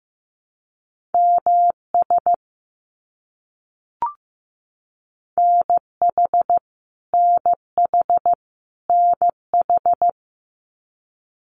Morse code, heard as a keyed tone, about 15 wpm, 700 Hz. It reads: MS NH NH NH
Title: US States Repeated 3x 15wpm